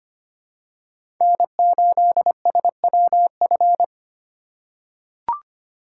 Morse code, heard as a keyed tone, about 25 wpm, 700 Hz. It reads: N8SWF
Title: USA Call Signs - 5 Characters - 2x Character Spacing - 25wpm